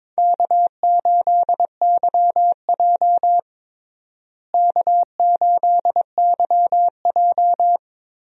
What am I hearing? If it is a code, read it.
Morse code, heard as a keyed tone, about 22 wpm, 700 Hz. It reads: K8YJ K8YJ
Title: USA Call Signs - 4 Characters - Repeated 3x - 22wpm